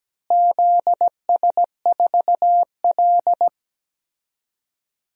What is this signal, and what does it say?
Morse code, heard as a keyed tone, about 17 wpm, 700 Hz. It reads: ZS4L
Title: DXCC-100 Call Signs - 4 Characters - Repeated 2x - 17wpm